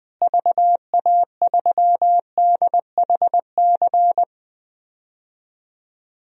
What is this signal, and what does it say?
Morse code, heard as a keyed tone, about 20 wpm, 700 Hz. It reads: VA3DHC
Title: DXCC-100 Call Signs - 6 Characters - Repeated 2x - 20wpm